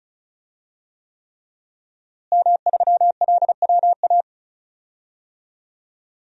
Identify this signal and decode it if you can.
Morse code, heard as a keyed tone, about 36 wpm, 700 Hz. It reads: M3LWA
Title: DXCC-100 Call Signs - 5 Characters - Repeated 3x - 35wpm